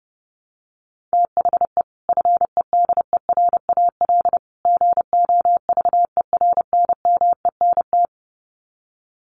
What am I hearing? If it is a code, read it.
Morse code, heard as a keyed tone, about 30 wpm, 700 Hz. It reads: THE FEDERAL GOVERNMENT
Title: Sets of 3 Words 30wpm